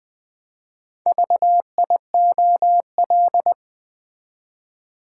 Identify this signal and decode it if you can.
Morse code, heard as a keyed tone, about 20 wpm, 700 Hz. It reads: VIOL
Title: Build A Word - 4 Letters At A Time - 8-Letter Words - 20wpm